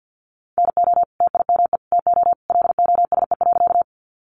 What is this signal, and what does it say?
Morse code, heard as a keyed tone, about 48 wpm, 700 Hz. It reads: NO TIME TO LOSE.
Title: Sentences from Top 1100 Words 50wpm